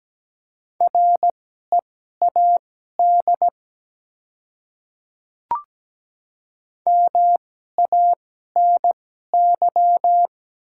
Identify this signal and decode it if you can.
Morse code, heard as a keyed tone, about 17 wpm, spaced as for 12 wpm, 700 Hz. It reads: READ MANY
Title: 2x Character Spacing - Top 300 Words - 17wpm